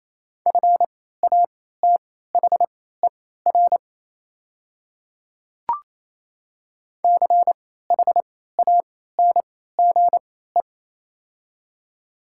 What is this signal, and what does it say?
Morse code, heard as a keyed tone, about 28 wpm, spaced as for 16 wpm, 700 Hz. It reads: FATHER CHANGE
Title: 3x Character Spacing - Top 200 Words - 28wpm